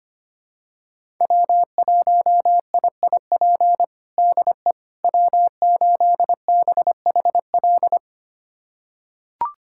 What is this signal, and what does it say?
Morse code, heard as a keyed tone, about 25 wpm, 700 Hz. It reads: W1IIP DE W8BHL
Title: USA Callsign DE Callsign - 5 Characters - 25wpm